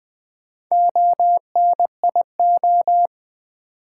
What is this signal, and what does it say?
Morse code, heard as a keyed tone, about 20 wpm, 690 Hz. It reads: ONIO
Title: Build A Word - 1 Letter At A Time - 5-Letter Words - 20wpm